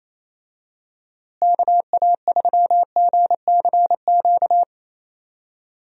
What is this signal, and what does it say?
Morse code, heard as a keyed tone, about 28 wpm, 700 Hz. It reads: KA3GCQ
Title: USA Call Signs - 6 Characters - Repeated 3x - 28wpm